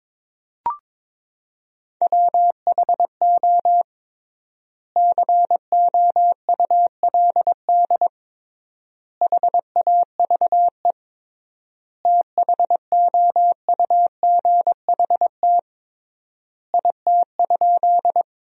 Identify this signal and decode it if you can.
Morse code, heard as a keyed tone, about 22 wpm, 700 Hz. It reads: WHO COULD HAVE THOUGHT IT?
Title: 3x Word-Spacing - Sentences from Top 300 Words - 22wpm